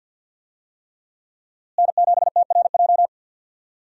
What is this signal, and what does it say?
Morse code, heard as a keyed tone, about 50 wpm, 700 Hz. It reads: N7TRJ
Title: USA Call Signs - 5 Characters - Repeated 3x - 50wpm